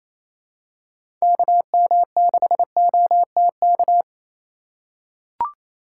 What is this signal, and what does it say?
Morse code, heard as a keyed tone, about 28 wpm, 700 Hz. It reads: KM6OTK
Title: USA Call Signs - 6 Characters - 2x Character Spacing - 28wpm